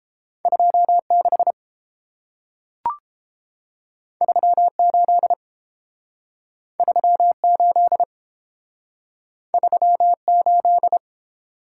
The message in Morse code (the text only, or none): 26 38 38 38